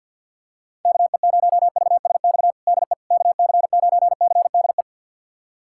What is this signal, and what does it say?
Morse code, heard as a keyed tone, about 51 wpm, 700 Hz. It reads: KE0VSX DE KC9CDE
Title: USA Callsign DE Callsign 50wpm